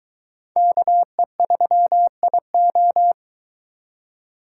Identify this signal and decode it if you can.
Morse code, heard as a keyed tone, about 23 wpm, 700 Hz. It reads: KE3IO